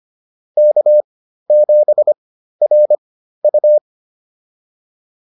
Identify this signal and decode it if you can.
Morse code, heard as a keyed tone, about 25 wpm, spaced as for 13 wpm, 600 Hz. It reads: K7RU